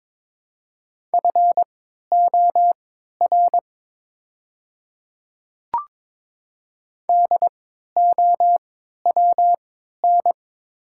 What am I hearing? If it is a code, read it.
Morse code, heard as a keyed tone, about 21 wpm, spaced as for 12 wpm, 700 Hz. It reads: FOR DOWN
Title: 3x Character Spacing - Top 300 Words - 22wpm